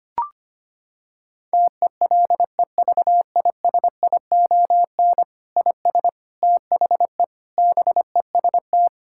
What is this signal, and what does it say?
Morse code, heard as a keyed tone, about 25 wpm, 700 Hz. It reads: TELEVISION IS THE BEST
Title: Sentences from Top 1000 Words - Review Part 6 - 25wpm